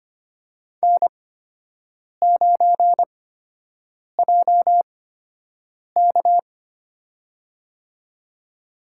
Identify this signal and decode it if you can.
Morse code, heard as a keyed tone, about 25 wpm, spaced as for 7 wpm, 700 Hz. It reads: N9JK
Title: USA Call Signs - 4 Characters - 8x Character Spacing - 25wpm